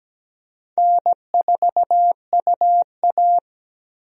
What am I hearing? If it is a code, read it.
Morse code, heard as a keyed tone, about 17 wpm, 700 Hz. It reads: N4UA
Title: USA Call Signs - 4 Characters - 17wpm